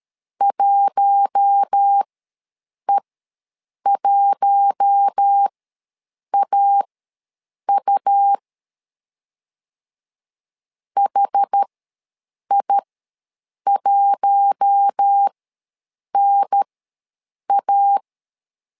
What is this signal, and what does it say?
Morse code, heard as a keyed tone, about 13 wpm, spaced as for 7 wpm, 800 Hz. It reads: 1E1AU HI1NA
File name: LEZIONI / 4 Q1 ws 12wpm 2129